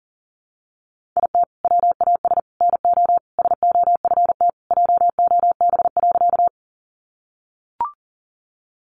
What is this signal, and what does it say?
Morse code, heard as a keyed tone, about 40 wpm, 700 Hz. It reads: IT WAS NO SOFT JOB.
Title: Sentences from Top 1500 Words 40wpm